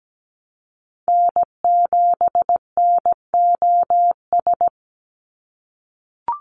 Morse code, heard as a keyed tone, about 17 wpm, 700 Hz. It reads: N7NOS
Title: USA Call Signs 17wpm